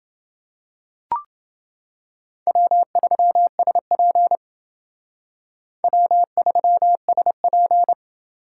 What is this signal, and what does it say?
Morse code, heard as a keyed tone, about 30 wpm, 700 Hz. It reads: W3SP W3SP